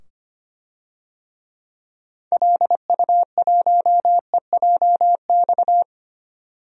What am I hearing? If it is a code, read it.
Morse code, heard as a keyed tone, about 25 wpm, 700 Hz. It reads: LU1EJX
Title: DXCC-100 Call Signs - 6 Characters - 25wpm